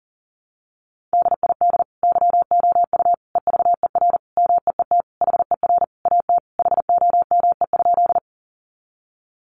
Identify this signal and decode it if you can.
Morse code, heard as a keyed tone, about 40 wpm, 700 Hz. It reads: DID YOU EVER MEET HER AT HOME?
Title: Sentences from Top 1100 Words 40wpm